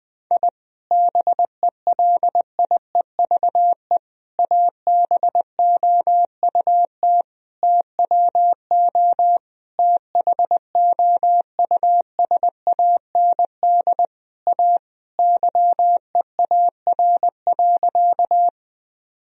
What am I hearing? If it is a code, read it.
Morse code, heard as a keyed tone, about 20 wpm, 700 Hz. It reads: I BELIEVE ABOUT TWO THOUSAND A YEAR.
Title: Sentences from Top 1000 Words - Review Part 3 - 20wpm